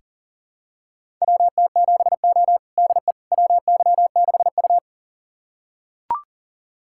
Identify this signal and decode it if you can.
Morse code, heard as a keyed tone, about 38 wpm, 700 Hz. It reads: WT7O DE WY6U
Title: USA Callsign DE Callsign - 4 Characters - 40wpm